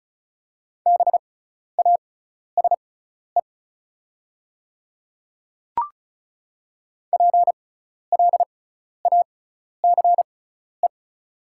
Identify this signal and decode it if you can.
Morse code, heard as a keyed tone, about 36 wpm, spaced as for 12 wpm, 690 Hz. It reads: BASE PLACE
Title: 6x Character Spacing - Top 500 Words - 35wpm